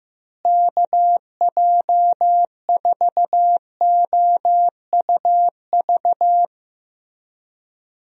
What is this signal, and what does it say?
Morse code, heard as a keyed tone, about 15 wpm, 700 Hz. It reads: KJ4OUV
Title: USA Call Signs - 6 Characters - Repeated 3x - 15wpm